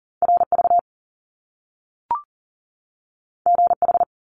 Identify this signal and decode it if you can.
Morse code, heard as a keyed tone, about 40 wpm, 700 Hz. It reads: RV GH
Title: Top Two Letter Combinations in Words 40wpm